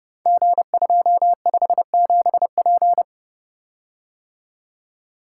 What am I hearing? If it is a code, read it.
Morse code, heard as a keyed tone, about 30 wpm, 700 Hz. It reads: G257P
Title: Code Groups - 5 Letter:Numbers - 30wpm